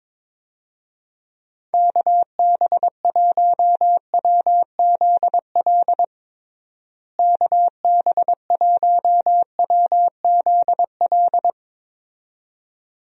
Morse code, heard as a keyed tone, about 22 wpm, 700 Hz. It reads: KB1WZL KB1WZL
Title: USA Call Signs - 6 Characters - Repeated 2x - 22wpm